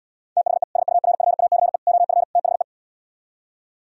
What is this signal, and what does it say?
Morse code, heard as a keyed tone, about 77 wpm, 700 Hz. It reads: THE SURFACE OF THE